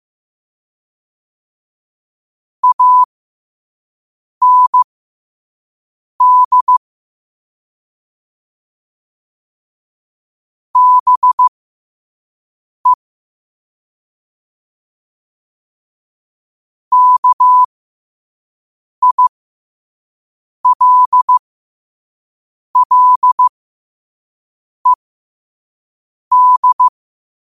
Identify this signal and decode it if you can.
Morse code, heard as a keyed tone, about 15 wpm, spaced as for 5 wpm, 1000 Hz. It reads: AND BE KILLED